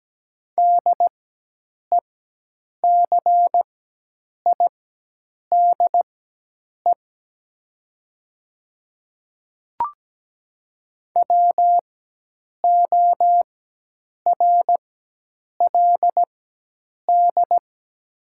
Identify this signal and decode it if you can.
Morse code, heard as a keyed tone, about 17 wpm, spaced as for 8 wpm, 700 Hz. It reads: DECIDE WORLD
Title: 4x Character Spacing - Top 500 Words - 17wpm